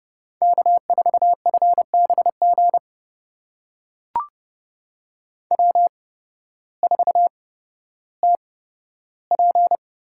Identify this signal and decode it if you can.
Morse code, heard as a keyed tone, about 30 wpm, spaced as for 14 wpm, 700 Hz. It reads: K4FBG W4TP
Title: USA Call Signs - 8x Character Spacing - 30wpm